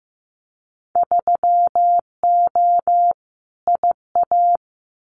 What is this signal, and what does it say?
Morse code, heard as a keyed tone, about 15 wpm, 700 Hz. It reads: 3O IA
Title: Winter Field Day Exchanges 15wpm